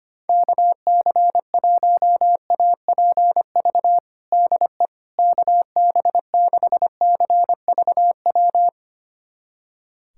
Morse code, heard as a keyed tone, about 25 wpm, 700 Hz. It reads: KC1APV DE KB6CVW
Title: USA Callsign DE Callsign 25wpm